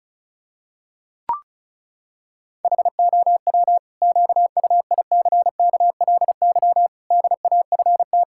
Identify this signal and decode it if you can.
Morse code, heard as a keyed tone, about 34 wpm, 700 Hz. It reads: HOW QUICKLY DAFT
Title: Pangram Challenge v2 - 35wpm